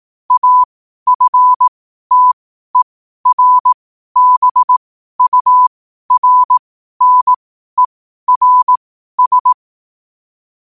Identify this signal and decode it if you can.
Morse code, heard as a keyed tone, about 18 wpm, spaced as for 12 wpm, 1000 Hz. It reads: AFTERBURNERS
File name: qotdfiles / 12wpm